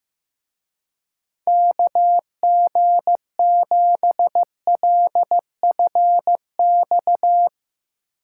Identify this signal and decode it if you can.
Morse code, heard as a keyed tone, about 16 wpm, 700 Hz. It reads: KG7LFX
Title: Speed Racing - USA 2x3 Call Signs 15wpm